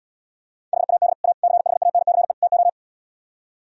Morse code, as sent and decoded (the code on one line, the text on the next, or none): .... .- -.. / .- / -.-. .... .- -. -.-. / - ---
HAD A CHANCE TO